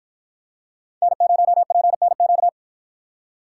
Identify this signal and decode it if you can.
Morse code, heard as a keyed tone, about 53 wpm, 700 Hz. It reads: N0PNQ